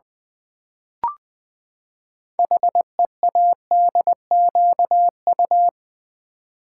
Morse code, heard as a keyed tone, about 20 wpm, 700 Hz. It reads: HEADQU
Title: Build A Word - 6 Letters At A Time - 12-Letter Words - 20wpm